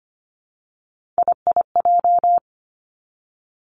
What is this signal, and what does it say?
Morse code, heard as a keyed tone, about 25 wpm, 700 Hz. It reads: IIJ
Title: Three Challenging Letter-Number Combinations 25wpm